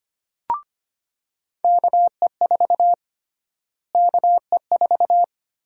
Morse code, heard as a keyed tone, about 25 wpm, 700 Hz. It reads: KE4 KE4
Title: USA Call Sign Prefixes - Repeated 2x - 25wpm